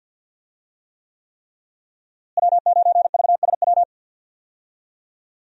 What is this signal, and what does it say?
Morse code, heard as a keyed tone, about 50 wpm, 700 Hz. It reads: W9VSW
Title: USA Call Signs - 5 Characters - Repeated 3x - 50wpm